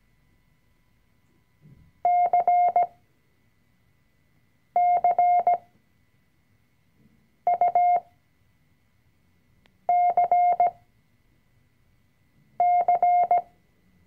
Morse code, heard as a keyed tone, about 17 wpm, spaced as for 4 wpm, 710 Hz. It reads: CCUCC
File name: CW-Online-Kurs / Anfänger / Lektion 004